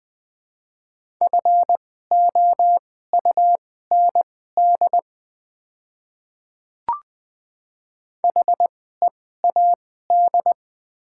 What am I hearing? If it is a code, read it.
Morse code, heard as a keyed tone, about 20 wpm, spaced as for 15 wpm, 700 Hz. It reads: FOUND HEAD
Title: 2x Character Spacing - Top 300 Words - 20wpm